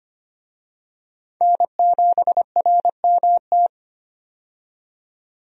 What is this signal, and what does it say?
Morse code, heard as a keyed tone, about 25 wpm, 700 Hz. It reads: N7RMT
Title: USA Call Signs - 5 Characters - 25wpm